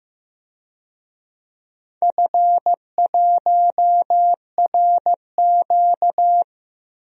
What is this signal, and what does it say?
Morse code, heard as a keyed tone, about 15 wpm, 700 Hz. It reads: F1RQ
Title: DXCC-100 Call Signs - 4 Characters - Repeated 2x - 15wpm